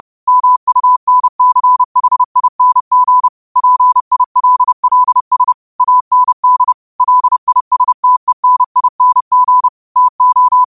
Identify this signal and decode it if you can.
Morse code, heard as a keyed tone, about 30 wpm, 1000 Hz. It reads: MUNCHING PILLS AND LISTENING TO
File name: qotdfiles / 30wpm